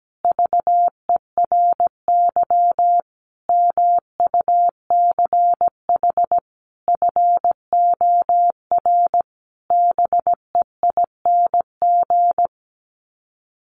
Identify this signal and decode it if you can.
Morse code, heard as a keyed tone, about 17 wpm, 700 Hz. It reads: VERY MUCH FOR BEING
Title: Sets of 5 Words Encore 17wpm